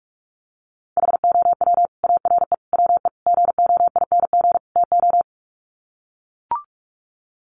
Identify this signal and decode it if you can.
Morse code, heard as a keyed tone, about 44 wpm, 700 Hz. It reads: HOW ARE WE GOING TO
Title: Sets of 5 Words Encore 45wpm